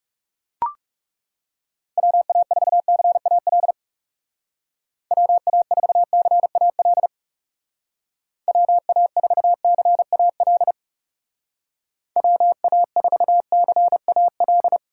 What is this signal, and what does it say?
Morse code, heard as a keyed tone, about 37 wpm, 700 Hz. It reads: WA4CAL WA4CAL WA4CAL WA4CAL
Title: Speed Racing - USA 2x3 Call Signs 30wpm